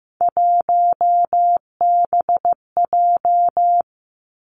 Morse code, heard as a keyed tone, about 15 wpm, 700 Hz. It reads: 1BJ
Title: Three Challenging Letter-Number Combinations 15wpm